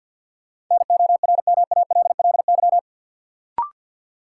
Morse code, heard as a keyed tone, about 53 wpm, 700 Hz. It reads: NORMALLY
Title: Sentences from Top 1000 Words - Review Part 6 - New Word Drill - 50wpm